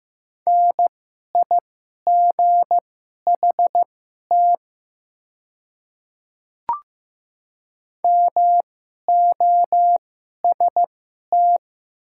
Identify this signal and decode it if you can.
Morse code, heard as a keyed tone, about 15 wpm, spaced as for 11 wpm, 700 Hz. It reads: NIGHT MOST
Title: 2x Character Spacing - Top 300 Words - 15wpm